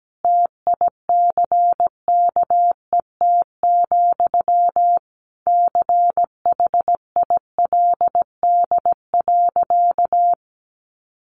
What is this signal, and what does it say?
Morse code, heard as a keyed tone, about 17 wpm, 700 Hz. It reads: TICKET, CHILD.
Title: Sentences from Top 1400 Words 17wpm